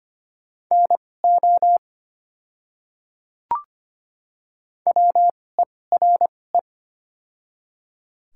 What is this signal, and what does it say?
Morse code, heard as a keyed tone, about 25 wpm, spaced as for 18 wpm, 700 Hz. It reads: NO WERE
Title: 2x Character Spacing - Top 200 Words - 25wpm